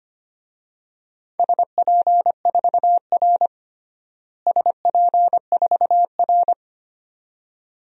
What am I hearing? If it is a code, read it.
Morse code, heard as a keyed tone, about 25 wpm, 700 Hz. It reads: SP4R SP4R
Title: DXCC-100 Call Signs - 4 Characters - Repeated 2x - 25wpm